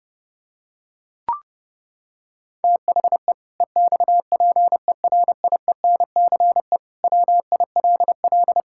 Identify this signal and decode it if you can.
Morse code, heard as a keyed tone, about 30 wpm, 700 Hz. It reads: THE EXPERIENCE WILL
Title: Sentences from Top 1000 Words - Review Part 5 - 30wpm